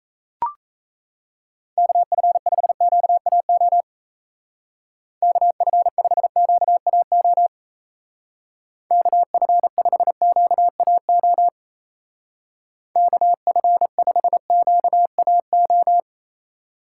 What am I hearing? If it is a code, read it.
Morse code, heard as a keyed tone, about 34 wpm, 700 Hz. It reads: KF5QAO KF5QAO KF5QAO KF5QAO
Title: Speed Racing - USA 2x3 Call Signs 28wpm